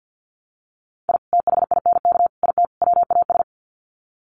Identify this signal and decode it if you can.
Morse code, heard as a keyed tone, about 50 wpm, 700 Hz. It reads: I THINK IT WAS
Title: Sets of 4 Words 50wpm